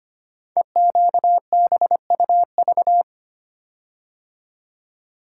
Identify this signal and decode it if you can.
Morse code, heard as a keyed tone, about 25 wpm, 700 Hz. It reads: EQBUV
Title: Code Groups - 5 Letters - 25wpm